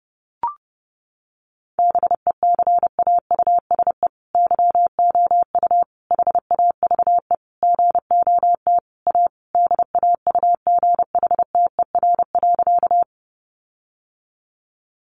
Morse code, heard as a keyed tone, about 30 wpm, 700 Hz. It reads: BECAUSE YOU HAVE GOT A DAUGHTER.
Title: Sentences from Top 1500 Words 30wpm